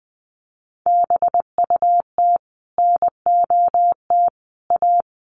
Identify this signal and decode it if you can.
Morse code, heard as a keyed tone, about 20 wpm, 700 Hz. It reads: BUT NOT A
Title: Sentences from Top 1400 Words 20wpm